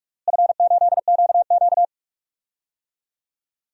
Morse code, heard as a keyed tone, about 43 wpm, 700 Hz. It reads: F8QQ
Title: DXCC-100 Call Signs - 4 Characters - Repeated 2x - 45wpm